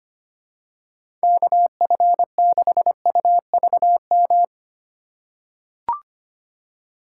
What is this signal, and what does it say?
Morse code, heard as a keyed tone, about 25 wpm, 700 Hz. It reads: KF6UVM
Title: USA Call Signs - 6 Characters - 3x Character Spacing - 25wpm